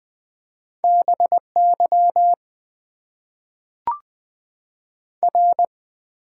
Morse code, heard as a keyed tone, about 20 wpm, 700 Hz. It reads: BY R
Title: Top 90 QSO Elements - 6x Character Spacing - 20wpm